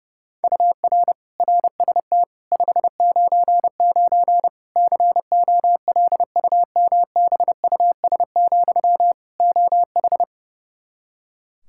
Morse code, heard as a keyed tone, about 30 wpm, 700 Hz. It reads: UR RST 599 COLUMBUS, OH
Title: K1USN SST QSO Practice 30wpm